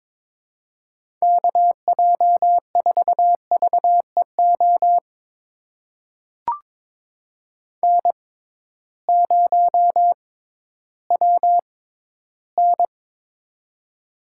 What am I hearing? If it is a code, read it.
Morse code, heard as a keyed tone, about 23 wpm, spaced as for 13 wpm, 700 Hz. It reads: KJ4VEO N0WN
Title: USA Call Signs - 6x Character Spacing - 22wpm